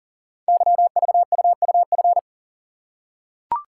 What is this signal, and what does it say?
Morse code, heard as a keyed tone, about 38 wpm, 700 Hz. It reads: YVUUF
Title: Code Groups - 5 Letter:Numbers - 40wpm